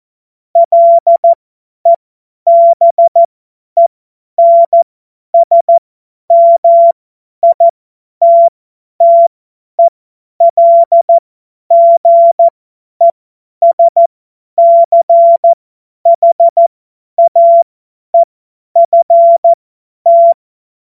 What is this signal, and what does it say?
Morse code, heard as a keyed tone, about 14 wpm, spaced as for 10 wpm, 680 Hz. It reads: LEBENSMITTELGESCHAEFT